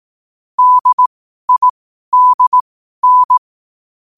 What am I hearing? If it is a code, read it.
Morse code, heard as a keyed tone, about 18 wpm, spaced as for 12 wpm, 1000 Hz. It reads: DIDN
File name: qotdfiles / 12wpm